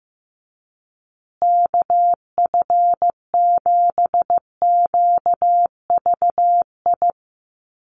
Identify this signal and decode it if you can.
Morse code, heard as a keyed tone, about 15 wpm, 700 Hz. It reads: KF7QVI
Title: USA Call Signs Repeated 3x 15wpm